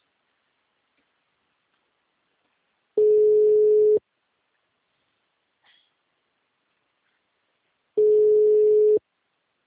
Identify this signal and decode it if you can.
Ringback tone (Europe, 425 Hz).
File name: out- -2011- - - .62229